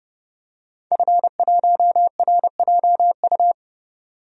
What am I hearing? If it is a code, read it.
Morse code, heard as a keyed tone, about 30 wpm, 700 Hz. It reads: F1RJU
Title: DXCC-100 Call Signs - 5 Characters - 30wpm